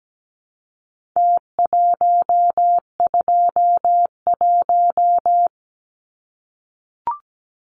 Morse code, heard as a keyed tone, about 17 wpm, 700 Hz. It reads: T121
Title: Numbers - Contest Style - 4-Digit 17wpm